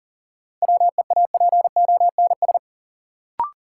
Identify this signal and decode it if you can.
Morse code, heard as a keyed tone, about 38 wpm, 700 Hz. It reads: WEAPONS
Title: Sets of 5 Words Encore - New Word Drill - 40wpm